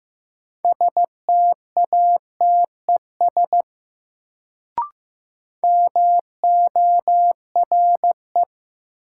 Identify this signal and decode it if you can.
Morse code, heard as a keyed tone, about 15 wpm, 700 Hz. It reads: STATES MORE
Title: Sets of 2 Words - New Word Drill - 15wpm